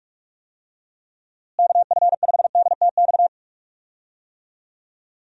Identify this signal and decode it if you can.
Morse code, heard as a keyed tone, about 46 wpm, 700 Hz. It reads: KF5DTX
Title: USA Call Signs - 6 Characters - Repeated 3x - 45wpm